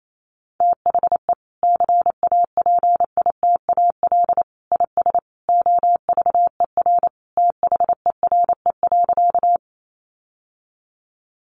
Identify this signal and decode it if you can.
Morse code, heard as a keyed tone, about 28 wpm, 700 Hz. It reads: THE CAPITAL IS OVER THERE.
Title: Sentences from Top 1000 Words 28wpm